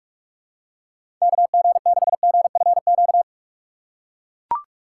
Morse code, heard as a keyed tone, about 42 wpm, 700 Hz. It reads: KG6GFQ
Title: USA Call Signs - 6 Characters - 3x Character Spacing - 45wpm